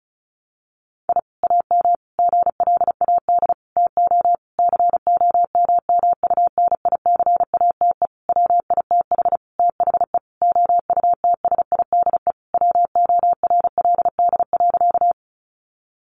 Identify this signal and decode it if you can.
Morse code, heard as a keyed tone, about 34 wpm, 700 Hz. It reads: I AM GLAD TO COMMUNICATE WITH THE OUTSIDE WORLD.
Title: Sentences from Top 1300 Words 35wpm